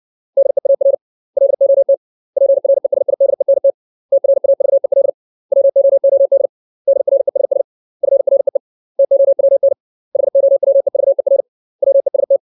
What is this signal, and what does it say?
Morse code, heard as a keyed tone, about 60 wpm, 550 Hz. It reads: DEAR LOT PRESIDENT TRAVEL WOOD DRESS FREE TOWN HOWEVER WEST
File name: 632 words / 632 Words 60WPM